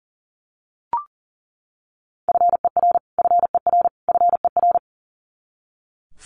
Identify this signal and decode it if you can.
Morse code, heard as a keyed tone, about 40 wpm, 700 Hz. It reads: FER FER FER
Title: Abbreviations Repeated 3x - 40wpm